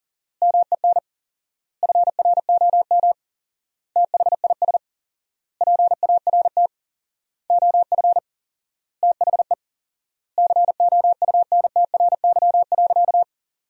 Morse code, heard as a keyed tone, about 40 wpm, 700 Hz. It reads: MEN FROM THIS PART OF THE COUNTRY.
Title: 4x Word-Spacing - Sentences from Top 300 Words - 40wpm